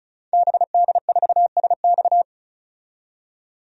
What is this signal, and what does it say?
Morse code, heard as a keyed tone, about 36 wpm, 700 Hz. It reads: BD4SX
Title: DXCC-100 Call Signs - 5 Characters - 35wpm